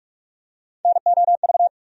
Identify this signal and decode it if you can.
Morse code, heard as a keyed tone, about 46 wpm, 700 Hz. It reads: NOV
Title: USA Call Sign Postfixes - 3 Letters 45wpm